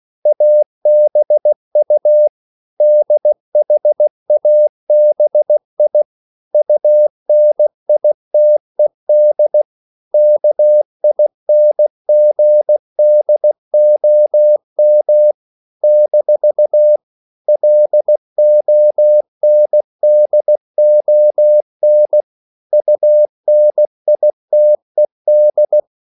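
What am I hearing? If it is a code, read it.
Morse code, heard as a keyed tone, about 16 wpm, 600 Hz. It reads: ABU DHABI UNITED KINGDOM - LONDON UNITED
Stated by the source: CW practice of national capitals at 16 words per minute, from ON6ZQ.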